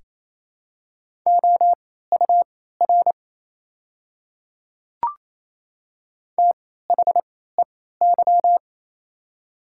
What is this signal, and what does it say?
Morse code, heard as a keyed tone, about 27 wpm, spaced as for 16 wpm, 700 Hz. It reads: OUR THEY
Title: 3x Character Spacing - Top 200 Words - 28wpm